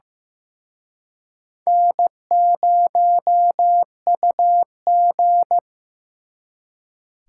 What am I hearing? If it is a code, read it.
Morse code, heard as a keyed tone, about 18 wpm, 700 Hz. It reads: N0UG
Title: Speed Racing - USA 1x2 Call Signs 15wpm